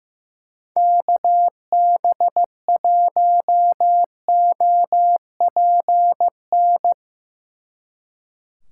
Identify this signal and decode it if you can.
Morse code, heard as a keyed tone, about 16 wpm, 700 Hz. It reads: KB1OPN
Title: Speed Racing - USA 2x3 Call Signs 15wpm